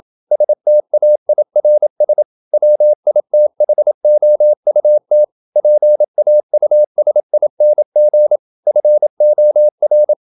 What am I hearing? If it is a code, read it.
Morse code, heard as a keyed tone, about 27 wpm, 600 Hz. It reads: STAIRS WITHOUT PAUSING FOR
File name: Never 0027wpm